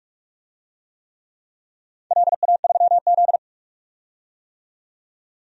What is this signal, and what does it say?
Morse code, heard as a keyed tone, about 48 wpm, 700 Hz. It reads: LA3Z
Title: DXCC-100 Call Signs - 4 Characters - 45wpm